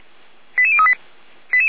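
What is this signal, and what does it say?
Five-tone selective call (ZVEI), heard as quick runs of five tones, tones of about 70 ms. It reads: 88038
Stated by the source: ZVEI five-tone selective call received on an SDR.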